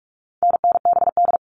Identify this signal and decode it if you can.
Morse code, heard as a keyed tone, about 44 wpm, 700 Hz. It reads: NNBD
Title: All Cut Numbers - 4-Digit 45wpm